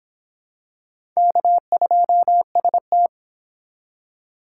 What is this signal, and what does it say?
Morse code, heard as a keyed tone, about 25 wpm, 700 Hz. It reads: K2ST